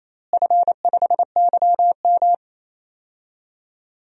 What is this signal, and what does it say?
Morse code, heard as a keyed tone, about 28 wpm, 700 Hz. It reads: F5YM
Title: DXCC-100 Call Signs - 4 Characters - Repeated 2x - 28wpm